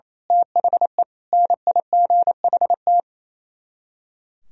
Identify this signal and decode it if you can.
Morse code, heard as a keyed tone, about 29 wpm, 700 Hz. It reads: THE NIGHT
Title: Speed-Racing - Sets of 5 Words 28wpm